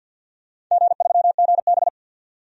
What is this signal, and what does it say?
Morse code, heard as a keyed tone, about 51 wpm, 700 Hz. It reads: G3GB
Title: DXCC-100 Call Signs - 4 Characters - 50wpm